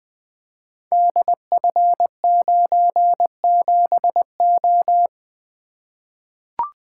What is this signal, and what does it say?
Morse code, heard as a keyed tone, about 21 wpm, 700 Hz. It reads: DF97O
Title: Code Groups - 5 Letter:Numbers - 20wpm